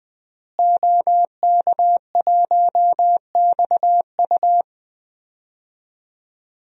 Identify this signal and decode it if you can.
Morse code, heard as a keyed tone, about 21 wpm, 700 Hz. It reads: OK1XU
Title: DXCC-100 Call Signs - 5 Characters - Repeated 2x - 20wpm